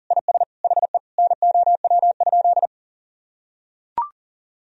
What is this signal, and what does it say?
Morse code, heard as a keyed tone, about 37 wpm, 700 Hz. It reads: IS HE NOW?
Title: Sentences from Top 100 Words and Abbreviations 40wpm